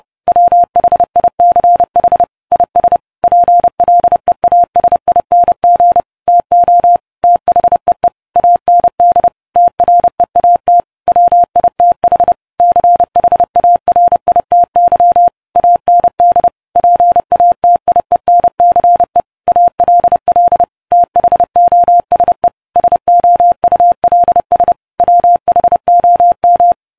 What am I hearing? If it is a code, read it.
Morse code, heard as a keyed tone, about 30 wpm, 700 Hz. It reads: WHICH IS PLEASING TO THEE AND TREAT WITH CHARITY AND PATIENCE ALL THOSE SOULS WHOM